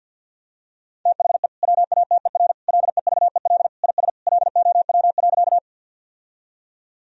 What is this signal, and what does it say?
Morse code, heard as a keyed tone, about 50 wpm, 700 Hz. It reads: THE WATER LEVEL IS LOW.
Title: Sentences from Top 1000 Words - Review Part 4 - 50wpm